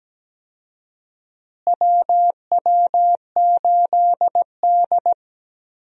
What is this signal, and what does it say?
Morse code, heard as a keyed tone, about 17 wpm, 700 Hz. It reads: WW8D
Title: USA Call Signs - 4 Characters - Repeated 2x - 17wpm